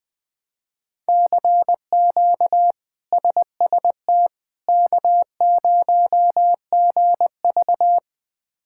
Morse code, heard as a keyed tone, about 20 wpm, 700 Hz. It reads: CQ SST K0GV
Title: K1USN SST QSO Practice - 20wpm